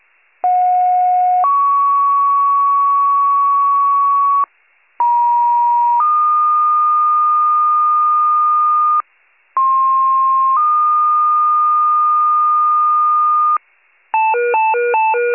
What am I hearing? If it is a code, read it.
Two-tone page, heard as a one-second tone, then a three-second tone, about 730 then 1120 Hz, about 950 then 1220 Hz, about 1060 then 1220 Hz. It is heard from 0.4 to 13.6 s.